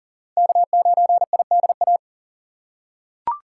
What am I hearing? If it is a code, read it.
Morse code, heard as a keyed tone, about 40 wpm, 700 Hz. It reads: K9IDA
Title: USA Call Signs - 5 Characters - 6x Character Spacing - 40wpm